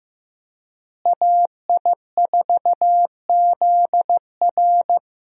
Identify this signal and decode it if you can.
Morse code, heard as a keyed tone, about 15 wpm, 700 Hz. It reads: AI4ZR
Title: USA Call Signs - 5 Characters - Repeated 3x - 15wpm